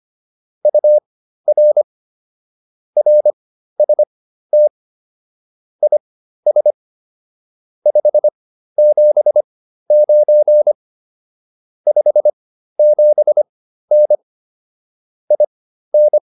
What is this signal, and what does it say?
Morse code, heard as a keyed tone, about 25 wpm, spaced as for 13 wpm, 600 Hz. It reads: UR RST IS 579 57N IN
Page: intermediate-practice-files